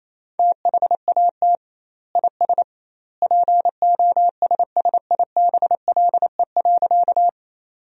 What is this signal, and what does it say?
Morse code, heard as a keyed tone, about 28 wpm, 700 Hz. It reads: THAT IS POSSIBLE.
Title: 2x Word-Spacing - Sentences from Top 500 Words - 28wpm